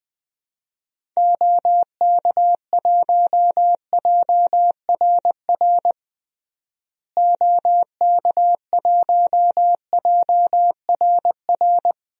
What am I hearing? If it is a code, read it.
Morse code, heard as a keyed tone, about 21 wpm, 700 Hz. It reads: OK1JRR OK1JRR
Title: DXCC-100 Call Signs - 6 Characters - Repeated 3x - 20wpm